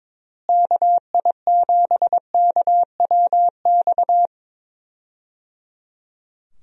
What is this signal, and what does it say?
Morse code, heard as a keyed tone, about 22 wpm, 700 Hz. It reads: KI7KWX
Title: USA Call Signs - 6 Characters - Repeated 3x - 22wpm